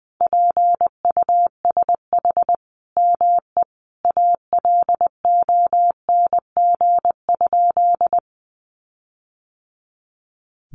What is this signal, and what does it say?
Morse code, heard as a keyed tone, about 20 wpm, 700 Hz. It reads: PUSH ME ALONG?
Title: Sentences from Top 1300 Words 20wpm